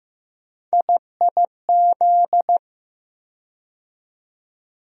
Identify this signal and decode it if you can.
Morse code, heard as a keyed tone, about 15 wpm, 700 Hz. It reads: IIZ
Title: USA Call Sign Postfixes - 3 Letters 15wpm